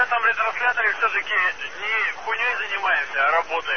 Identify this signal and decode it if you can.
Emergency Alert System (SAME) tones, heard as harsh buzzing bursts, with speech. It is heard from 0.6 to 3.2 s.